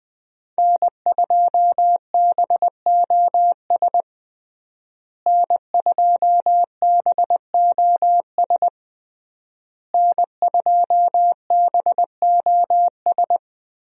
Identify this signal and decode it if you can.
Morse code, heard as a keyed tone, about 20 wpm, 700 Hz. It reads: N2BOS N2BOS N2BOS
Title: USA Call Signs - 5 Characters - Repeated 3x - 20wpm